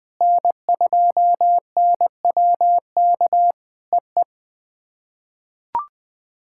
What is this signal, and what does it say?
Morse code, heard as a keyed tone, about 20 wpm, 700 Hz. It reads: N2NWK EE
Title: POTA QSO Practice - 20wpm